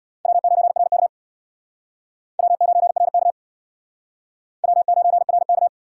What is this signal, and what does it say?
Morse code, heard as a keyed tone, about 61 wpm, 700 Hz. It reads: W9RK W9RK W9RK
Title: Speed Racing - USA 1x2 Call Signs 50wpm